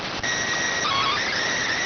SSTV (slow-scan television), PD 290.